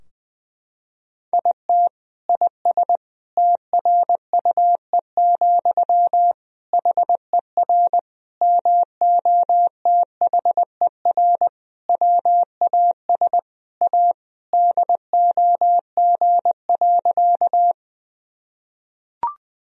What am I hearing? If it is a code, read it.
Morse code, heard as a keyed tone, about 20 wpm, 700 Hz. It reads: IT IS TRUE, HER MOTHER WAS A DOG.
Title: Sentences from Top 1000 Words - Review Part 2 - 20wpm